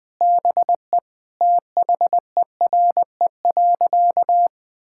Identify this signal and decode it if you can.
Morse code, heard as a keyed tone, about 20 wpm, 700 Hz. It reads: BE THERE.
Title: Sentences with 3-Word Phrases - 20wpm